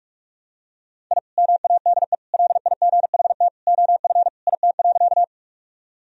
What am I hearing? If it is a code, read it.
Morse code, heard as a keyed tone, about 46 wpm, 700 Hz. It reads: I MADE LIGHT OF IT.
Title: Sentences from Top 1000 Words - Review Part 1 - 45wpm